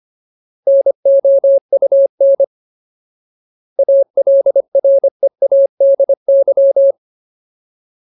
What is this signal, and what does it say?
Morse code, heard as a keyed tone, about 25 wpm, 550 Hz. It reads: NOUN ALREADY
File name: Ultimate Word List / 25WPM Ultimate